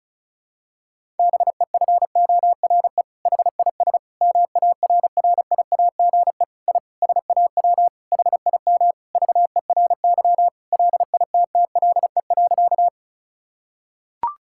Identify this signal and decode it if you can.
Morse code, heard as a keyed tone, about 34 wpm, 700 Hz. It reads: BEFORE HIS MARRIAGE I SAW HIM VERY LITTLE.
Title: Sentences from Top 1000 Words - Review Part 5 - 35wpm